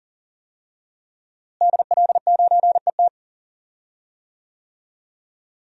Morse code, heard as a keyed tone, about 40 wpm, 700 Hz. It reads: DL9ET
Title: DXCC-100 Call Signs - 5 Characters - Repeated 3x - 40wpm